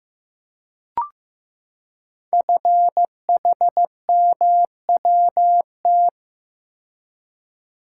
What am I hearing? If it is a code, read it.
Morse code, heard as a keyed tone, about 15 wpm, 700 Hz. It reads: FHMWT
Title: Code Groups - 5 Letters - 15wpm